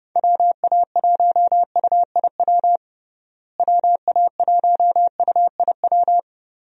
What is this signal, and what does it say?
Morse code, heard as a keyed tone, about 30 wpm, 700 Hz. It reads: WA1UIW WA1UIW
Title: USA Call Signs - 6 Characters - Repeated 3x - 30wpm